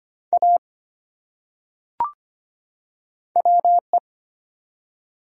Morse code, heard as a keyed tone, about 25 wpm, 700 Hz. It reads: A WE
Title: Top 100 Words 25wpm